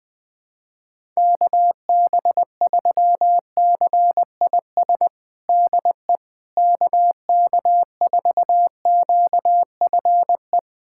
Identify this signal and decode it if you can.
Morse code, heard as a keyed tone, about 20 wpm, 700 Hz. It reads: KB3CIS DE KK4QFE
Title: USA Callsign DE Callsign - 6 Characters - 20wpm